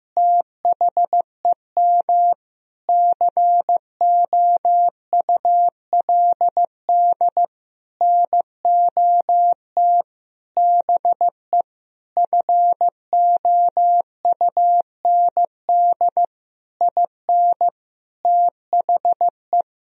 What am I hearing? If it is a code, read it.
Morse code, heard as a keyed tone, about 15 wpm, 700 Hz. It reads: THEM COULD NOT BE FOUND IN THE
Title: Sentences from Top 1000 Words - Review Part 1 - 15wpm